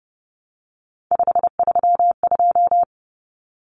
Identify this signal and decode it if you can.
Morse code, heard as a keyed tone, about 30 wpm, 700 Hz. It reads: 532
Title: Numbers - 3-Digit 30wpm